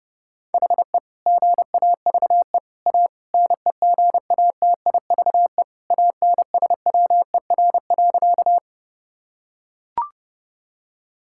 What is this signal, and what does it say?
Morse code, heard as a keyed tone, about 32 wpm, 700 Hz. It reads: HE GAVE A NEGATIVE ANSWER.
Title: Sentences from Top 1000 Words - Review Part 6 - 30wpm